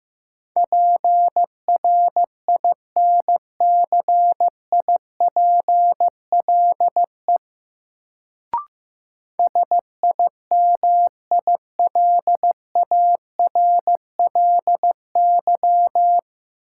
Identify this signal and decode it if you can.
Morse code, heard as a keyed tone, about 15 wpm, 700 Hz. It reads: PRINCIPLE SIMILARLY
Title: Sentences from Top 1400 Words - New Word Drill - 15wpm